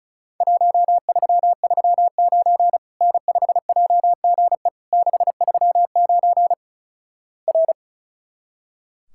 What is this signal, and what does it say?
Morse code, heard as a keyed tone, about 36 wpm, 700 Hz. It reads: 1339 N5JGE 639 R
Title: Worked All Europe DX Contest - QTC Exchange - 35wpm